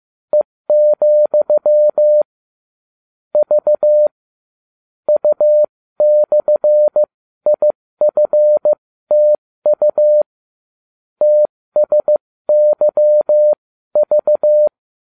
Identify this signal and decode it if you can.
Morse code, heard as a keyed tone, about 15 wpm, spaced as for 13 wpm, 600 Hz. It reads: E, V U/IFTU TSYV